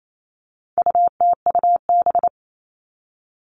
Morse code, heard as a keyed tone, about 28 wpm, 700 Hz. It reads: UTUB
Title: All Cut Numbers - 4-Digit 28wpm